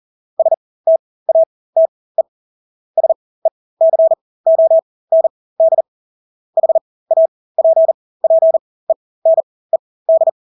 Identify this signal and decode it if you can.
Morse code, heard as a keyed tone, about 40 wpm, spaced as for 20 wpm, 650 Hz. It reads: STATE SECOND HAPPENED
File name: Engelske ord 40-20 wpm